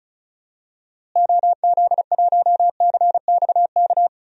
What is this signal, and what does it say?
Morse code, heard as a keyed tone, about 36 wpm, 700 Hz. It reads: OZ1CXK
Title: DXCC-100 Call Signs - 6 Characters - Repeated 3x - 35wpm